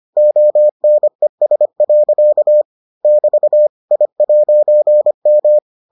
Morse code, heard as a keyed tone, about 25 wpm, 600 Hz. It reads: ONES. <BT> I'M